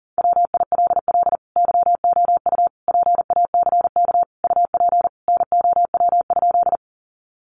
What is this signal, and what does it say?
Morse code, heard as a keyed tone, about 40 wpm, 700 Hz. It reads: WILL YOU PACK UP NOW?
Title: Sentences from Top 1300 Words 40wpm